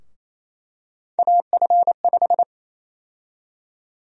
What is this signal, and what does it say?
Morse code, heard as a keyed tone, about 28 wpm, 700 Hz. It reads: AF5
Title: USA Call Sign Prefixes - 28wpm